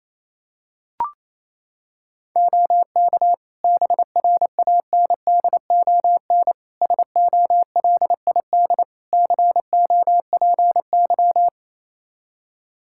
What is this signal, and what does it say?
Morse code, heard as a keyed tone, about 28 wpm, 700 Hz. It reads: OK BRANDON SOLID COPY
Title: K1USN SST QSO Practice 28wpm